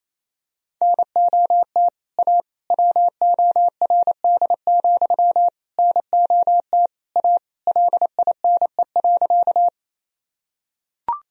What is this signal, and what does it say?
Morse code, heard as a keyed tone, about 27 wpm, 700 Hz. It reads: NOT A WORD, NOT A LINE.
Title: Sentences from Top 1000 Words - Review Part 1 - 28wpm